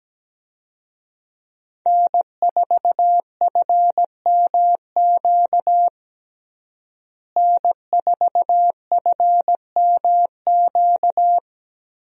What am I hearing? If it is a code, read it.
Morse code, heard as a keyed tone, about 17 wpm, 700 Hz. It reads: N4FMQ N4FMQ
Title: USA Call Signs - 5 Characters - Repeated 3x - 17wpm